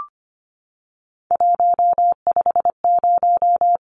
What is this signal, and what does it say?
Morse code, heard as a keyed tone, about 25 wpm, 700 Hz. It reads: 150
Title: Numbers - 3-Digit 25wpm